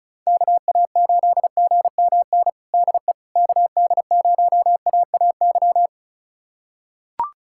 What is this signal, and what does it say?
Morse code, heard as a keyed tone, about 36 wpm, 700 Hz. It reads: KA8GMN DE KD0AAY
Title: USA Callsign DE Callsign 35wpm